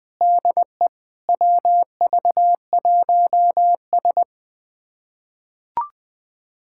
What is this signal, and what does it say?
Morse code, heard as a keyed tone, about 20 wpm, 700 Hz. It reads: DE WV1S
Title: USA Callsign DE Callsign - 4 Characters - 20wpm